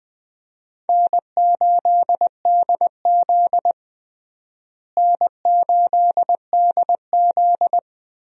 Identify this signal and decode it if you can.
Morse code, heard as a keyed tone, about 21 wpm, 700 Hz. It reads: N8DZ N8DZ
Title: USA Call Signs - 4 Characters - Repeated 3x - 20wpm